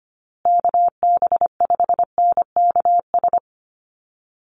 Morse code, heard as a keyed tone, about 25 wpm, 700 Hz. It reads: KB5NKS
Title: USA Call Signs Repeated 3x 25wpm